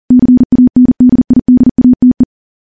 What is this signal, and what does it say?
Morse code, heard as a keyed tone, about 40 wpm, 250 Hz. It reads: CANDIDATE
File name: morse / morse-training-40WPM